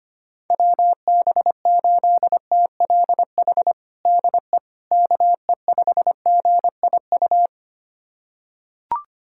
Morse code, heard as a keyed tone, about 25 wpm, 700 Hz. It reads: WB8TLH DE KE5GIU
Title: USA Callsign DE Callsign - 6 Characters - 25wpm